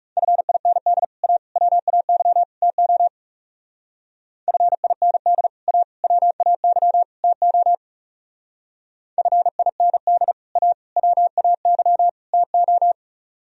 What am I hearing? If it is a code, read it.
Morse code, heard as a keyed tone, about 37 wpm, 700 Hz. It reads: FIND A WAY TO FIND A WAY TO FIND A WAY TO
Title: Speed-Racing - Sets of 4 Words 30wpm